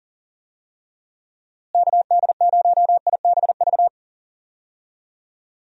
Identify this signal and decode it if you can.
Morse code, heard as a keyed tone, about 40 wpm, 700 Hz. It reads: KD0IBV
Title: USA Call Signs - 6 Characters - Repeated 2x - 40wpm